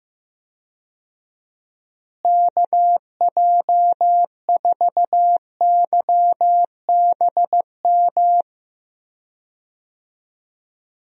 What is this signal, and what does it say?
Morse code, heard as a keyed tone, about 15 wpm, 700 Hz. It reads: KJ4YBM
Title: USA Call Signs - 6 Characters - Repeated 2x - 15wpm